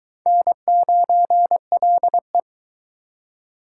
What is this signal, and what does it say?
Morse code, heard as a keyed tone, about 23 wpm, 700 Hz. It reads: N9LE